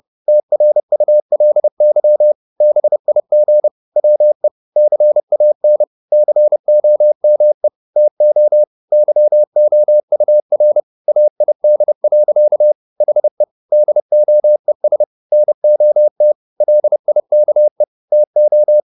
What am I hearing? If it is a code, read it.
Morse code, heard as a keyed tone, about 30 wpm, 600 Hz. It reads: TRULY BIG WE CAN COME TO YOUR AID. HE DOES NOT LIKE TO
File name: the old man and the sea / the old man and the sea 1 30wpm